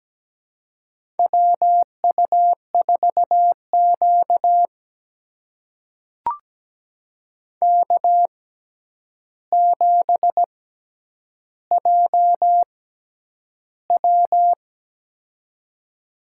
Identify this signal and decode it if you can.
Morse code, heard as a keyed tone, about 17 wpm, spaced as for 9 wpm, 700 Hz. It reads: WU4Q K7JW
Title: USA Call Signs - 4 Characters - 6x Character Spacing - 17wpm